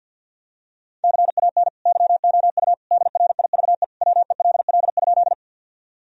Morse code, heard as a keyed tone, about 46 wpm, 700 Hz. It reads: CAN YOU DRIVE WELL?
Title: Sentences from Top 1000 Words - Review Part 2 - 50wpm